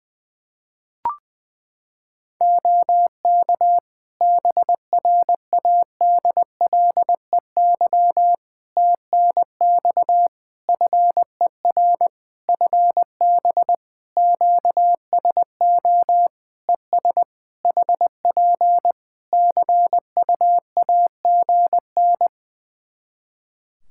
Morse code, heard as a keyed tone, about 20 wpm, 700 Hz. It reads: OK BRADLEY TNX FER FB QSO ES HP CUAGN
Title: Practice QSO - Exchange 4 - Ending - 20wpm